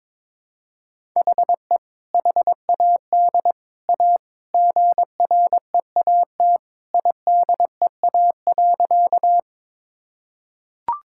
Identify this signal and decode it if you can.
Morse code, heard as a keyed tone, about 22 wpm, 700 Hz. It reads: HE HAD A GREAT IDEA.
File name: practice / 22wpm / Sentences with 3-Word Phrases - 22wpm